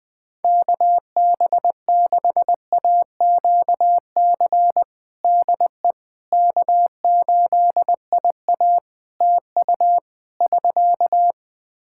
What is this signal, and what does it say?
Morse code, heard as a keyed tone, about 20 wpm, 700 Hz. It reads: KB6AQC DE K8IA TU <SK>
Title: K1USN SST QSO Practice 20wpm